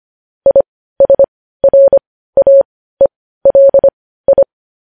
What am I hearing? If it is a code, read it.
Morse code, heard as a keyed tone, about 25 wpm, spaced as for 15 wpm, 550 Hz. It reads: ISRAELI
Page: news headlines in morse code at 15 wpm